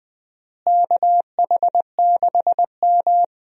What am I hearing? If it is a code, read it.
Morse code, heard as a keyed tone, about 20 wpm, 700 Hz. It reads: KH6M
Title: USA Call Signs - 4 Characters - Repeated 2x - 20wpm